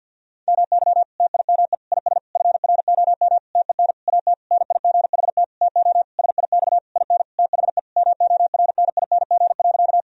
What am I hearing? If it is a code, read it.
Morse code, heard as a keyed tone, about 50 wpm, 700 Hz. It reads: MY TIME IS FROM TEN AT NIGHT TO SIX IN THE MORNING.
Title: Sentences from Top 1000 Words - Review Part 2 - 50wpm